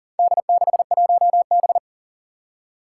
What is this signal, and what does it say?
Morse code, heard as a keyed tone, about 40 wpm, 700 Hz. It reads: D61B
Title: Four Challenging Letter-Number Combinations 40wpm